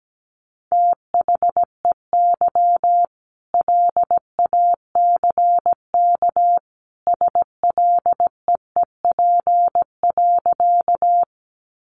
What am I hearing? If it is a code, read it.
Morse code, heard as a keyed tone, about 17 wpm, 700 Hz. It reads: THEY LACK SLEEP.
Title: Sentences from Top 1100 Words 17wpm